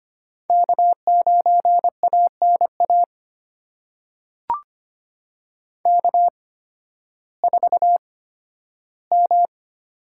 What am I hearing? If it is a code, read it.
Morse code, heard as a keyed tone, about 25 wpm, spaced as for 13 wpm, 700 Hz. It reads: K9ANA K4M
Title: USA Call Signs - 5 Characters - 8x Character Spacing - 25wpm